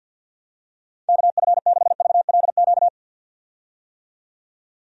Morse code, heard as a keyed tone, about 50 wpm, 700 Hz. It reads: KF6VLQ
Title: USA Call Signs - 6 Characters - 50wpm